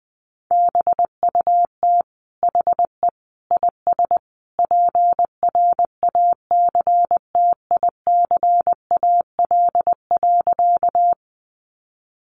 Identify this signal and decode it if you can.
Morse code, heard as a keyed tone, about 20 wpm, 700 Hz. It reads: BUT HE IS PRACTICAL.
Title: Sentences from Top 1400 Words 20wpm